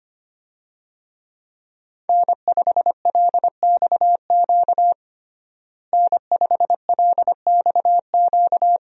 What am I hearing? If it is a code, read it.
Morse code, heard as a keyed tone, about 25 wpm, 700 Hz. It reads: N5LXQ N5LXQ
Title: USA Call Signs - 5 Characters - Repeated 2x - 25wpm